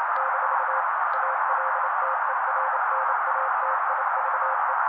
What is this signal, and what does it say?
Morse code, heard as a keyed tone, about 28 wpm, 530 Hz. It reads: - ALTERNATIVE
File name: daily / 27wpm noise